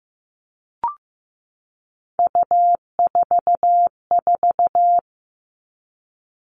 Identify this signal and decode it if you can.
Morse code, heard as a keyed tone, about 15 wpm, 700 Hz. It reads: U44
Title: All Cut Numbers - 3-Digit 15wpm